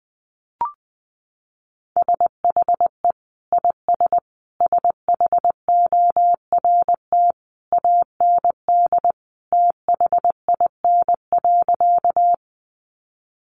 Morse code, heard as a keyed tone, about 19 wpm, 700 Hz. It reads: SHE IS SHORT AND THIN.
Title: Sentences from Top 1200 Words 20wpm